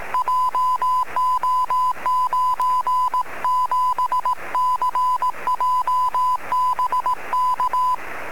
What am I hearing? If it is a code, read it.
Morse code, heard as a keyed tone, about 18 wpm, 1010 Hz. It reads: JO97CJ BK
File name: sm7lcb / 26-sm1mut-23cm